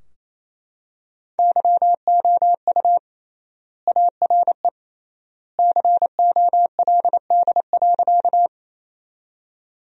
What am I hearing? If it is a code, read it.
Morse code, heard as a keyed tone, about 28 wpm, 700 Hz. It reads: YOU ARE COLD.
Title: 3x Word-Spacing - Sentences from Top 400 Words - 28wpm